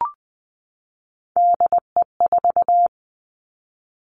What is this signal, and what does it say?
Morse code, heard as a keyed tone, about 20 wpm, 700 Hz. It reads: DE4
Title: Three Challenging Letter-Number Combinations 20wpm